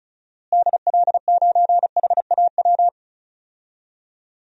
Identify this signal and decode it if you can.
Morse code, heard as a keyed tone, about 36 wpm, 700 Hz. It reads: DL9HAW
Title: DXCC-100 Call Signs - 6 Characters - Repeated 3x - 35wpm